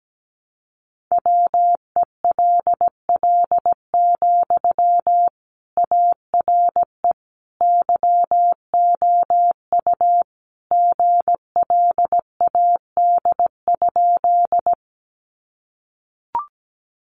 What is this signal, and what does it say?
Morse code, heard as a keyed tone, about 17 wpm, 700 Hz. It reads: WELL, ARE YOU GLAD?
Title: Sentences from Top 800 Words 17wpm